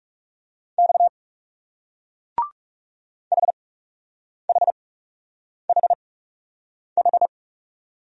Morse code, heard as a keyed tone, about 41 wpm, spaced as for 8 wpm, 700 Hz. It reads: X HHHH